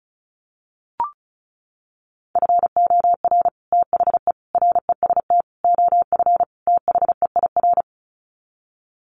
Morse code, heard as a keyed tone, about 34 wpm, 700 Hz. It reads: FOR THE REST OF THEIR
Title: Sets of 5 Words Encore2 35wpm